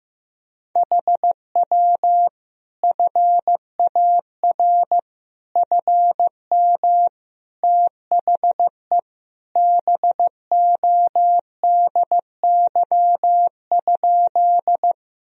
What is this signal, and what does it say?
Morse code, heard as a keyed tone, about 15 wpm, 700 Hz. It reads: HW FAR FM THE BODY?
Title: Sentences from Top 400 Words and Abbreviations 15wpm